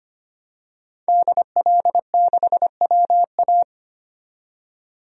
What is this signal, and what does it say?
Morse code, heard as a keyed tone, about 25 wpm, 700 Hz. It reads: DL6WA
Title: DXCC-100 Call Signs - 5 Characters - 25wpm